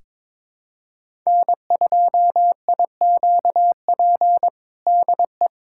Morse code, heard as a keyed tone, about 22 wpm, 700 Hz. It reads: N2IQP DE
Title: USA Callsign DE Callsign - 5 Characters - 22wpm